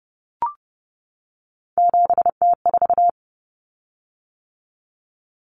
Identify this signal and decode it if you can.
Morse code, heard as a keyed tone, about 30 wpm, 700 Hz. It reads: 7T4
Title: Common Cut Numbers - 3-Digit 30wpm